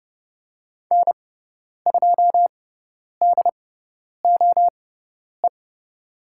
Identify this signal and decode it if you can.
Morse code, heard as a keyed tone, about 30 wpm, spaced as for 10 wpm, 700 Hz. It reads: N2DOE
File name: Lesson 16g - Letter c - Call Signs 30wpm